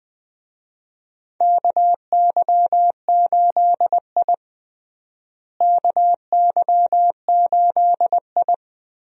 Morse code, heard as a keyed tone, about 20 wpm, 700 Hz. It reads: KY8I KY8I
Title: USA Call Signs - 4 Characters - Repeated 3x - 20wpm